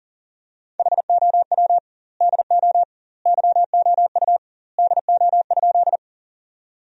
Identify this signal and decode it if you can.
Morse code, heard as a keyed tone, about 40 wpm, 700 Hz. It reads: HOW DO YOU DO?
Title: 2x Word-Spacing - Sentences from Top 100 Words - 40wpm